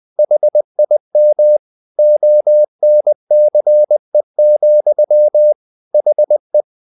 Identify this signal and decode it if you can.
Morse code, heard as a keyed tone, about 20 wpm, 600 Hz. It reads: HIM ONCE, HE
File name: the old man and the sea / the old man and the sea part 1 20wpm